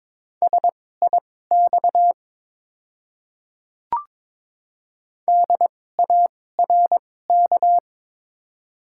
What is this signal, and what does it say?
Morse code, heard as a keyed tone, about 21 wpm, spaced as for 16 wpm, 700 Hz. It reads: SIX DARK
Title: 2x Character Spacing - Top 500 Words - 22wpm